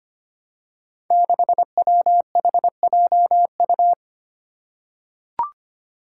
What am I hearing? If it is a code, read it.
Morse code, heard as a keyed tone, about 25 wpm, 700 Hz. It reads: 6WHJU
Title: Five Challenging Letter-Number Combinations 25wpm